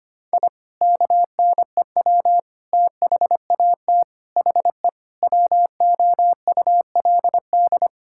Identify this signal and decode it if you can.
Morse code, heard as a keyed tone, about 25 wpm, 700 Hz. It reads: I KNEW THAT HE WOULD
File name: practice / 25wpm / Sentences with 2-Word Phrases - 25wpm